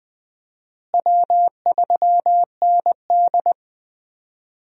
Morse code, heard as a keyed tone, about 20 wpm, 700 Hz. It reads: W3ND
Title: USA Call Signs - 4 Characters - Repeated 3x - 20wpm